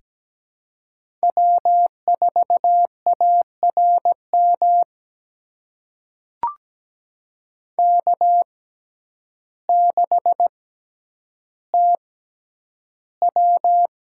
Morse code, heard as a keyed tone, about 17 wpm, spaced as for 9 wpm, 700 Hz. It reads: W4ARM K6TW
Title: USA Call Signs - 5 Characters - 6x Character Spacing - 17wpm